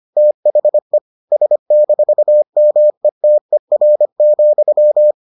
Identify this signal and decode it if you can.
Morse code, heard as a keyed tone, about 25 wpm, 600 Hz. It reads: THE S-METER,